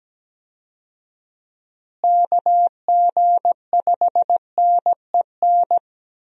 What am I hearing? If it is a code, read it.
Morse code, heard as a keyed tone, about 17 wpm, 700 Hz. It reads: KG5NEN
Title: USA Call Signs - 6 Characters - Repeated 2x - 17wpm